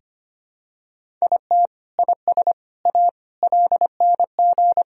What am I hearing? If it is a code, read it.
Morse code, heard as a keyed tone, about 25 wpm, 700 Hz. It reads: IT IS A LNG
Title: Sentences from Top 300 Words and Abbreviations 25wpm